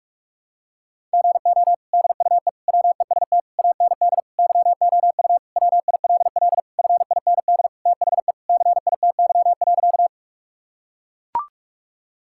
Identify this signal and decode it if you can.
Morse code, heard as a keyed tone, about 42 wpm, 700 Hz. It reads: GO DUE WEST AND YOU WILL FIND THE CITY.
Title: Sentences from Top 1000 Words - Review Part 6 - 45wpm